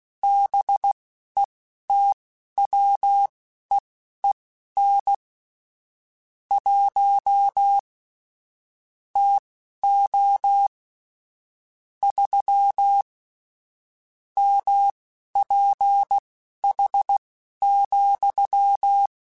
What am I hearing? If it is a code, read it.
Morse code, heard as a keyed tone, about 16 wpm, spaced as for 11 wpm, 800 Hz. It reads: BETWEEN 1 TO 3 MPH,
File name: Weather Sets / Weather Sets-mp3s / Weather 5 5 @10